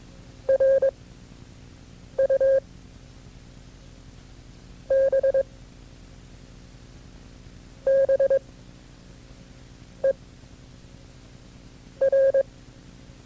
Morse code, heard as a keyed tone, about 22 wpm, 540 Hz. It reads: R U B B E R
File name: FreeFiles / CW-practice / KB6MT-tape2-side1